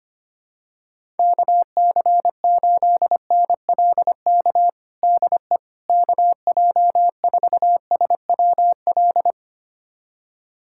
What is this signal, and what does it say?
Morse code, heard as a keyed tone, about 25 wpm, 700 Hz. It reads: KC8NLK DE KJ4SWL
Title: USA Callsign DE Callsign 25wpm